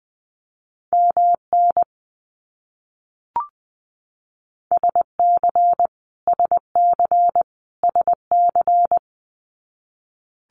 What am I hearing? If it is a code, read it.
Morse code, heard as a keyed tone, about 20 wpm, 700 Hz. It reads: MN SC SC SC
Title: US States Repeated 3x 20wpm